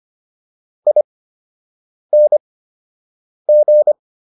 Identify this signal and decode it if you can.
Morse code, heard as a keyed tone, about 25 wpm, spaced as for 7 wpm, 600 Hz. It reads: ING